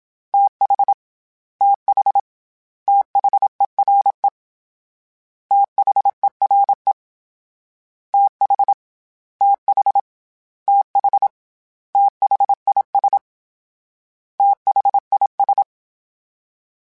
Morse code, heard as a keyed tone, about 27 wpm, 800 Hz. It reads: TH TH THERE THERE TH TH TH THIS THIS
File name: TH / TH mp3s / TH Prefixes 3x2 @25